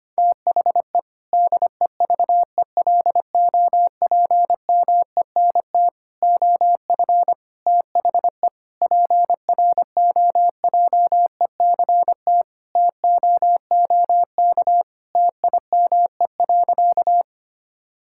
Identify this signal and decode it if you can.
Morse code, heard as a keyed tone, about 25 wpm, 700 Hz. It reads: THE DEVELOPMENT OF THE PROJECT TOOK TIME.
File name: practice / 25wpm / Sentences with 3-Word Phrases - 25wpm